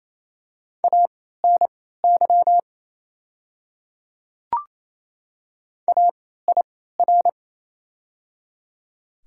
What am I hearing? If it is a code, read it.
Morse code, heard as a keyed tone, about 28 wpm, spaced as for 16 wpm, 690 Hz. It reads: ANY AIR
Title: 3x Character Spacing - Top 200 Words - 28wpm